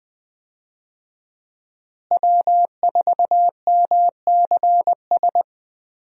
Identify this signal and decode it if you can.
Morse code, heard as a keyed tone, about 20 wpm, 700 Hz. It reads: W4MCS
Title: USA Call Signs - 5 Characters - Repeated 3x - 20wpm